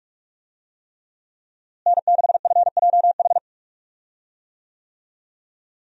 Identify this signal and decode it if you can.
Morse code, heard as a keyed tone, about 44 wpm, 700 Hz. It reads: N6FJH
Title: USA Call Signs - 5 Characters - Repeated 3x - 45wpm